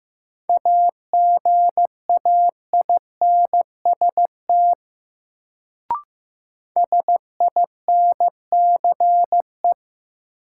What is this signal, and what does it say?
Morse code, heard as a keyed tone, about 15 wpm, 700 Hz. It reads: AGAINST SINCE
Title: Sets of 2 Words - New Word Drill - 15wpm